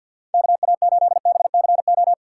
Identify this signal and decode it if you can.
Morse code, heard as a keyed tone, about 51 wpm, 700 Hz. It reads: KA8BCO
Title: USA Call Signs - 6 Characters - 4x Character Spacing - 50wpm